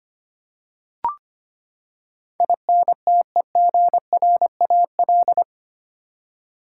Morse code, heard as a keyed tone, about 25 wpm, 700 Hz. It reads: INTEGRAL
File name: Eight Letter Words - 25wpm